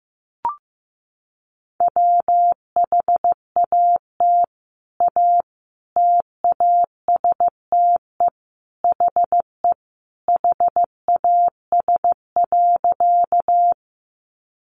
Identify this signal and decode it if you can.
Morse code, heard as a keyed tone, about 15 wpm, 700 Hz. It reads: WHAT A TASTE HE HAS.
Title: Sentences from Top 1100 Words 15wpm